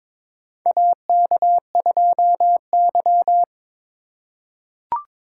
Morse code, heard as a keyed tone, about 22 wpm, 700 Hz. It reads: AK2Y
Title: USA Call Signs - 4 Characters - 4x Character Spacing - 22wpm